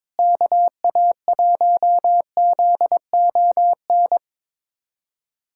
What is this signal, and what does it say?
Morse code, heard as a keyed tone, about 22 wpm, 700 Hz. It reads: KA1ZON
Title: USA Callsign DE Callsign - 6 Characters - 22wpm